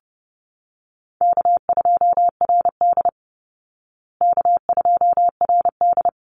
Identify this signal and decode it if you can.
Morse code, heard as a keyed tone, about 30 wpm, 700 Hz. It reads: K2RD K2RD
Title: Call Signs Repeated 3x - 30wpm